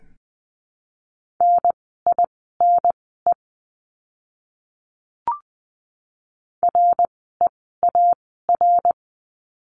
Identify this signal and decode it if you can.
Morse code, heard as a keyed tone, about 20 wpm, spaced as for 14 wpm, 700 Hz. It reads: NINE REAR
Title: 061 Lesson 11f - Letter h - Words 20wpm